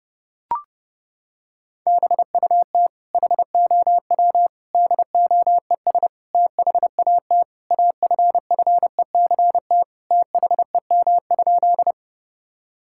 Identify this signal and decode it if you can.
Morse code, heard as a keyed tone, about 32 wpm, 700 Hz. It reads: BUT HOW DOES THAT AFFECT THEM?
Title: Sentences from Top 1000 Words - Review Part 6 - 30wpm